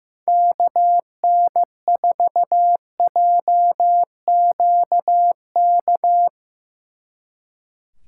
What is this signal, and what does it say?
Morse code, heard as a keyed tone, about 17 wpm, 700 Hz. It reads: KN4JQK
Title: Speed Racing - USA 2x3 Call Signs 15wpm